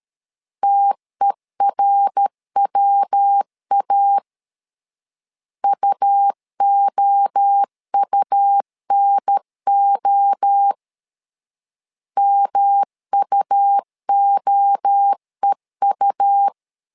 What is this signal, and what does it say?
Morse code, heard as a keyed tone, about 13 wpm, 800 Hz. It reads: TERWA UOUNO MUOEU
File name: LEZIONI / 3 MWRO / 3 MWRO 12wpm 1730